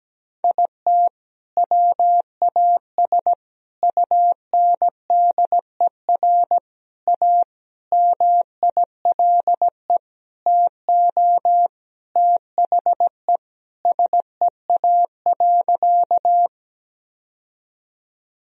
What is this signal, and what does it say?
Morse code, heard as a keyed tone, about 17 wpm, 700 Hz. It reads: IT WAS UNDER A MILE TO THE SEA.
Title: Sentences from Top 1000 Words - Review Part 1 - 17wpm